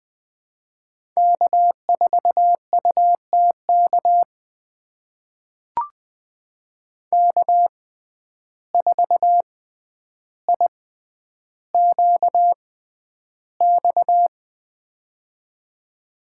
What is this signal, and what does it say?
Morse code, heard as a keyed tone, about 20 wpm, spaced as for 10 wpm, 700 Hz. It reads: K4UTK K4IQX
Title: USA Call Signs - 5 Characters - 6x Character Spacing - 20wpm